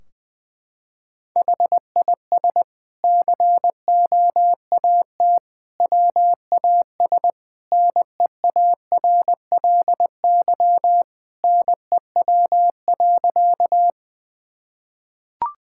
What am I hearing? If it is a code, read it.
Morse code, heard as a keyed tone, about 20 wpm, 700 Hz. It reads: HIS COAT WAS NEARLY NEW.
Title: Sentences from Top 1600 Words 20wpm